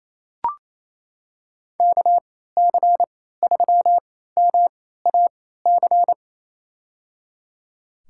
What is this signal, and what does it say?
Morse code, heard as a keyed tone, about 27 wpm, spaced as for 16 wpm, 700 Hz. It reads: KC3MAC
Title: USA Call Signs - 6 Characters - 3x Character Spacing - 28wpm